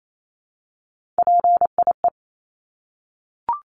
Morse code, heard as a keyed tone, about 28 wpm, 700 Hz. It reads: PIE
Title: Top 1751-2000 Words 28wpm